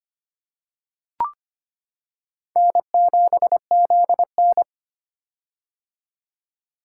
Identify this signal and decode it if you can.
Morse code, heard as a keyed tone, about 25 wpm, 700 Hz. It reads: N7ZN
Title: K1USN SST QSO Practice - 25wpm